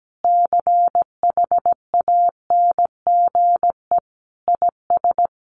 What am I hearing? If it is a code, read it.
Morse code, heard as a keyed tone, about 17 wpm, 700 Hz. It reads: CHANGE IS
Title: Sentences from Top 1500 Words 17wpm